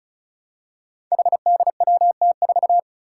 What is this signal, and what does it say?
Morse code, heard as a keyed tone, about 36 wpm, 700 Hz. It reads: HDWT4
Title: Five Challenging Letter-Number Combinations 35wpm